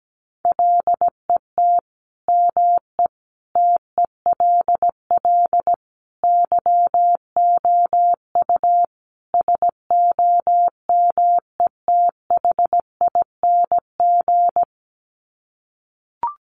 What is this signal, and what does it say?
Morse code, heard as a keyed tone, about 17 wpm, 700 Hz. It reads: LET ME TELL YOU SOMETHING
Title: Sets of 5 Words 17wpm